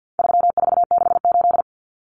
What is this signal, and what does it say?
Morse code, heard as a keyed tone, about 50 wpm, 700 Hz. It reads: 3468
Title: Numbers - 4-Digit 50wpm